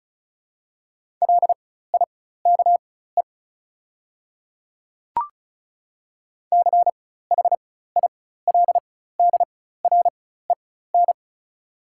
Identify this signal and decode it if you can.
Morse code, heard as a keyed tone, about 36 wpm, spaced as for 16 wpm, 690 Hz. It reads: LIKE CHILDREN
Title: 4x Character Spacing - Top 300 Words - 35wpm